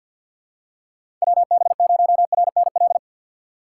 Morse code, heard as a keyed tone, about 48 wpm, 700 Hz. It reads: WB0RNL
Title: USA Call Signs - 6 Characters - 6x Character Spacing - 50wpm